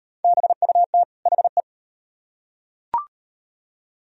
Morse code, heard as a keyed tone, about 35 wpm, 700 Hz. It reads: BUT HE